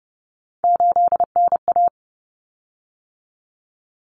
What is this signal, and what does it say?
Morse code, heard as a keyed tone, about 30 wpm, 700 Hz. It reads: 8NA
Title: Common Cut Numbers - 3-Digit 30wpm